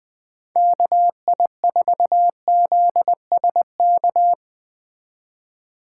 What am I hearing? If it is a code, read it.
Morse code, heard as a keyed tone, about 21 wpm, 700 Hz. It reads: KI4ZSK